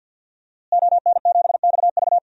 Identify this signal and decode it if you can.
Morse code, heard as a keyed tone, about 52 wpm, 700 Hz. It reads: ON7XV
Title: DXCC-100 Call Signs - 5 Characters - Repeated 2x - 50wpm